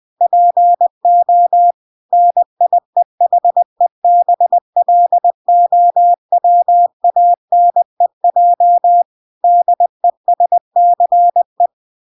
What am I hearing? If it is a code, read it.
Morse code, heard as a keyed tone, about 20 wpm, 700 Hz. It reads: PO NIEHEBLOWANEJ DESCE